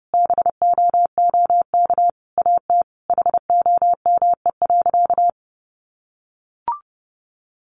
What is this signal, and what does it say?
Morse code, heard as a keyed tone, about 30 wpm, 700 Hz. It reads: BOOK AT HOME.
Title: Sentences from Top 1400 Words 30wpm